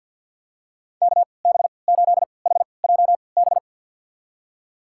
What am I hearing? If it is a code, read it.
Morse code, heard as a keyed tone, about 52 wpm, spaced as for 29 wpm, 700 Hz. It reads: KB8HJB
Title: USA Call Signs - 6 Characters - 3x Character Spacing - 50wpm